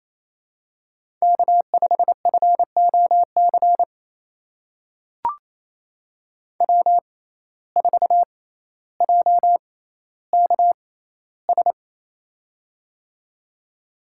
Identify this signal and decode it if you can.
Morse code, heard as a keyed tone, about 28 wpm, spaced as for 14 wpm, 700 Hz. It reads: K5FOC W4JKS
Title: USA Call Signs - 5 Characters - 6x Character Spacing - 28wpm